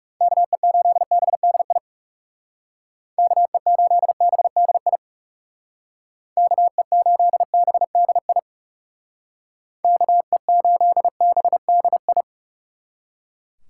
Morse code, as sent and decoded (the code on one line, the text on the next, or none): -.- . ---.. -... -.. .. / -.- . ---.. -... -.. .. / -.- . ---.. -... -.. .. / -.- . ---.. -... -.. ..
KE8BDI KE8BDI KE8BDI KE8BDI